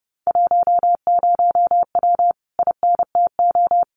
Morse code, heard as a keyed tone, about 30 wpm, 700 Hz. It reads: 10W INTO
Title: Common QSO Phrases Encore 30wpm